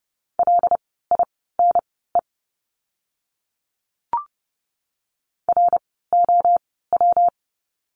Morse code, heard as a keyed tone, about 30 wpm, spaced as for 17 wpm, 700 Hz. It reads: LINE ROW
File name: course / 30:10wpm / 115 Lesson 19f - Letter w - Words 30wpm